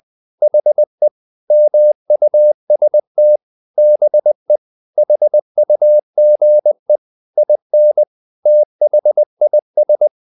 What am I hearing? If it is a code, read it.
Morse code, heard as a keyed tone, about 20 wpm, 600 Hz. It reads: HE MUST BE HUGE IN THIS
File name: the old man and the sea / the old man and the sea part 1 20wpm